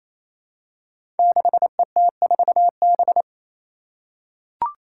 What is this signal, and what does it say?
Morse code, heard as a keyed tone, about 27 wpm, 700 Hz. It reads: 6ET4B
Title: Five Challenging Letter-Number Combinations 28wpm